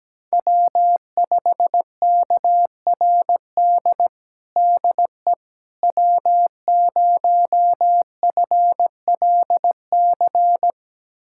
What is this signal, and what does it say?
Morse code, heard as a keyed tone, about 17 wpm, 700 Hz. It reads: W5KRD DE W0FLC
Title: USA Callsign DE Callsign - 5 Characters - 17wpm